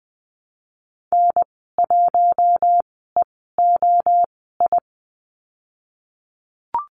Morse code, heard as a keyed tone, about 20 wpm, spaced as for 15 wpm, 700 Hz. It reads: N1EOI
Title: 048 Lesson 9g - Number 4 - Call Signs 20wpm